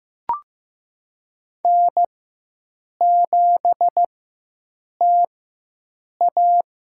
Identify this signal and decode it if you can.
Morse code, heard as a keyed tone, about 15 wpm, spaced as for 7 wpm, 700 Hz. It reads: N7TA
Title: USA Call Signs - 4x Character Spacing - 15wpm